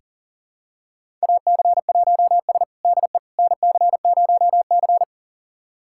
Morse code, heard as a keyed tone, about 40 wpm, 700 Hz. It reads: AC1S DE NC0C
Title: USA Callsign DE Callsign - 4 Characters - 40wpm